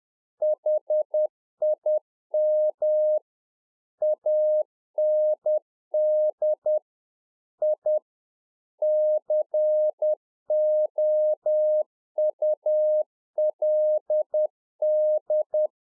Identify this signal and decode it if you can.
Morse code, heard as a keyed tone, about 10 wpm, 610 Hz. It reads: HIM AND I COULD
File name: UNFOGETTABLE QSO 10WPM